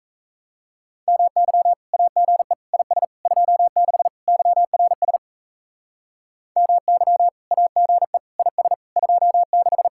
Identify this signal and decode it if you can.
Morse code, heard as a keyed tone, about 38 wpm, 700 Hz. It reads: MY AGE IS 26 YRS MY AGE IS 26